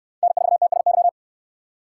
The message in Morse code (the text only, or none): N4TSY